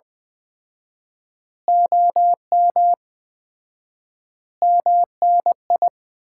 Morse code, heard as a keyed tone, about 20 wpm, 700 Hz. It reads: OM MNI
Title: Common QSO Phrases - 4x Word-Spacing - 20wpm